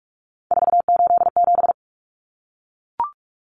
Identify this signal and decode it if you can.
Morse code, heard as a keyed tone, about 44 wpm, 700 Hz. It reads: 487
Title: Numbers - 3-Digit 45wpm